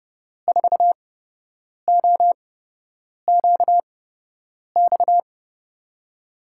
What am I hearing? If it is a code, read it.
Morse code, heard as a keyed tone, about 30 wpm, spaced as for 8 wpm, 700 Hz. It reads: 4OQX
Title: USA Call Signs - 8x Character Spacing - 30wpm